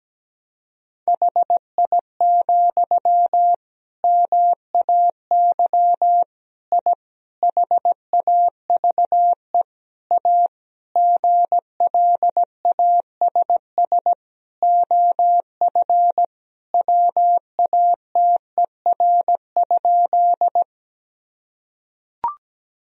Morse code, heard as a keyed tone, about 17 wpm, 700 Hz. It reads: HI, MAY I HAVE A GLASS OF WATER?
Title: Sentences from Top 1000 Words - Review Part 3 - 17wpm